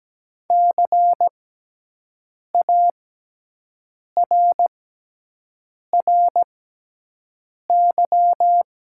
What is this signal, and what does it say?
Morse code, heard as a keyed tone, about 17 wpm, spaced as for 6 wpm, 700 Hz. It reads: CARRY
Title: 6x Character Spacing - Top 300 Words - 17wpm